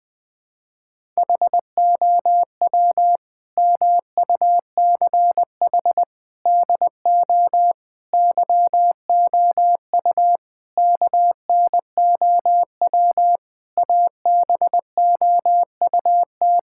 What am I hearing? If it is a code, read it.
Morse code, heard as a keyed tone, about 21 wpm, 700 Hz. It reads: HOW MUCH DO YOU KNOW ABOUT
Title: Sentences from Top 1000 Words - Review Part 6 - 20wpm